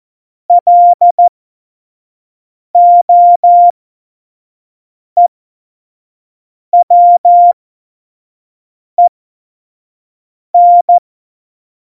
Morse code, heard as a keyed tone, about 14 wpm, spaced as for 5 wpm, 700 Hz. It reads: LOEWEN
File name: Morsekurs / Witze / Witz 14. August 2018, 25 F70